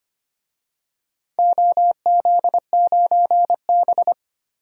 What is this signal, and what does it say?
Morse code, heard as a keyed tone, about 25 wpm, 700 Hz. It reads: OZ9B
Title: DXCC-100 Call Signs - 4 Characters - Repeated 2x - 25wpm